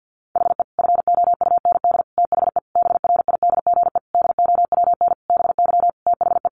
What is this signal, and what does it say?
Morse code, heard as a keyed tone, about 50 wpm, 700 Hz. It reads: HE FOUND THE BRIDGE DOWN BY THE
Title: Sentences from Top 1200 Words 50wpm